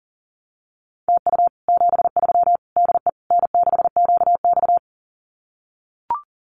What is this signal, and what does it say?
Morse code, heard as a keyed tone, about 37 wpm, 700 Hz. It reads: TU 73 DE N6QX
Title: Common QSO Phrases Encore 40wpm